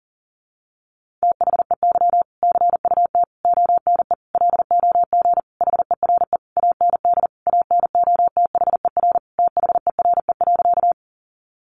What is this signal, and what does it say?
Morse code, heard as a keyed tone, about 40 wpm, 700 Hz. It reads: THEY CUT ONE LOG HERE AND ANOTHER THERE.
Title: Sentences from Top 1400 Words 40wpm